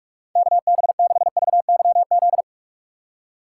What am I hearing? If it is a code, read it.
Morse code, heard as a keyed tone, about 56 wpm, 700 Hz. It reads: KB6VYZ
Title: Speed Racing - USA 2x3 Call Signs 45wpm